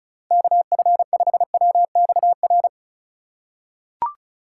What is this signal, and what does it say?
Morse code, heard as a keyed tone, about 36 wpm, 700 Hz. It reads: KF5WXR
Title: USA Call Signs - 6 Characters - 2x Character Spacing - 35wpm